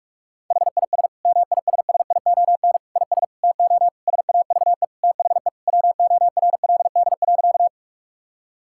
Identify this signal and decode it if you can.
Morse code, heard as a keyed tone, about 41 wpm, 700 Hz. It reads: HIS MISSION IS TO SAVE THE WORLD.
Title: Sentences from Top 1600 Words 45wpm